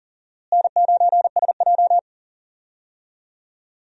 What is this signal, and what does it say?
Morse code, heard as a keyed tone, about 39 wpm, 700 Hz. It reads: N9SJ
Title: USA Call Signs - 4 Characters - Repeated 3x - 40wpm